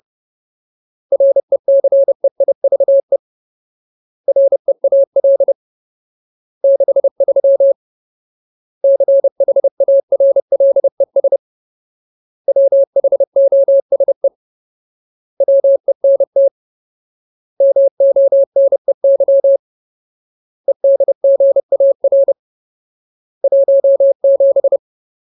Morse code, heard as a keyed tone, about 30 wpm, 550 Hz. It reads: RECEIVE REAL 63 CHARLES WHOSE WENT MONEY EDGAR 17
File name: Ultimate Word List / 30WPM Ultimate